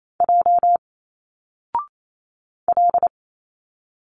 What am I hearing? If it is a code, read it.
Morse code, heard as a keyed tone, about 28 wpm, 700 Hz. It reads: J L
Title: Single Letters 28wpm